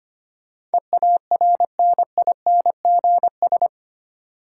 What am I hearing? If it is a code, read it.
Morse code, heard as a keyed tone, about 25 wpm, 700 Hz. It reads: EARNINGS
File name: 8 Letter Words - First 3 Letters Given - 25wpm